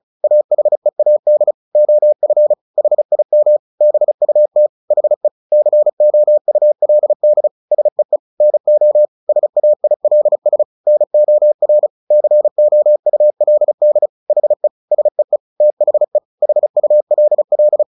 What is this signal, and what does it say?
Morse code, heard as a keyed tone, about 34 wpm, 600 Hz. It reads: AHEAD OF HIM BUT HE COULD SEE NO SAILS NOR COULD HE SEE THE HULL
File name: the old man and the sea 2 35wpm